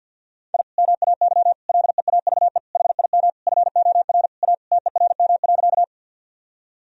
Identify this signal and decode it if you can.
Morse code, heard as a keyed tone, about 50 wpm, 700 Hz. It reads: I MAY LEAVE HIM FOR A TERM.
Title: Sentences from Top 1000 Words - Review Part 4 - 50wpm